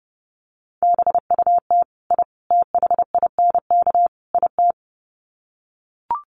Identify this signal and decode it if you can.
Morse code, heard as a keyed tone, about 30 wpm, 700 Hz. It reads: BUT I THINK IT
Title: Sets of 4 Words 30wpm